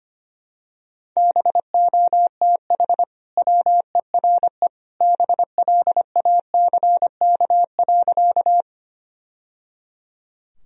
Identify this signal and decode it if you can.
Morse code, heard as a keyed tone, about 25 wpm, 700 Hz. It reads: BOTH WERE BLACK.
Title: Sentences from Top 1000 Words - Review Part 2 - 25wpm